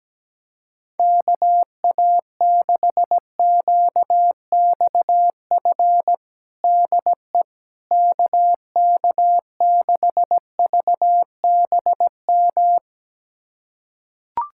Morse code, heard as a keyed tone, about 17 wpm, 700 Hz. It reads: KA6QXF DE KK6VBM
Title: USA Callsign DE Callsign - 6 Characters - 17wpm